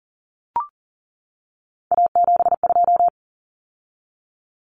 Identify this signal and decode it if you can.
Morse code, heard as a keyed tone, about 40 wpm, 700 Hz. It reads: A72
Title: Common Cut Numbers - 3-Digit 40wpm